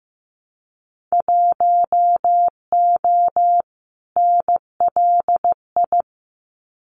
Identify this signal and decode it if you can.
Morse code, heard as a keyed tone, about 15 wpm, 700 Hz. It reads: 1O NLI
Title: Winter Field Day Exchanges 15wpm